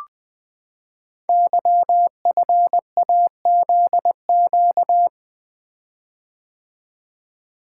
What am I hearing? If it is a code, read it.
Morse code, heard as a keyed tone, about 20 wpm, 700 Hz. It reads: YFAZQ
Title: Code Groups - 5 Letter:Numbers - 20wpm